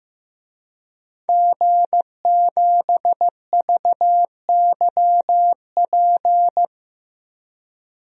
Morse code, heard as a keyed tone, about 15 wpm, 700 Hz. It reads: G7VYP
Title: DXCC-100 Call Signs - 5 Characters - Repeated 2x - 15wpm